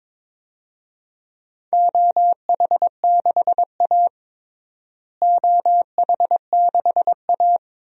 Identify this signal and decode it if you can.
Morse code, heard as a keyed tone, about 22 wpm, 700 Hz. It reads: OH6A OH6A
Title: DXCC-100 Call Signs - 4 Characters - Repeated 2x - 22wpm